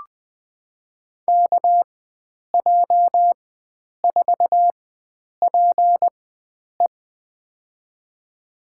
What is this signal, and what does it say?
Morse code, heard as a keyed tone, about 20 wpm, spaced as for 9 wpm, 700 Hz. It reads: KJ4PE
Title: USA Call Signs - 4x Character Spacing - 20wpm